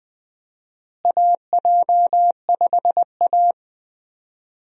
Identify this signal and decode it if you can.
Morse code, heard as a keyed tone, about 20 wpm, 700 Hz. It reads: AJ5A
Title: USA Call Signs - 4 Characters - Repeated 2x - 20wpm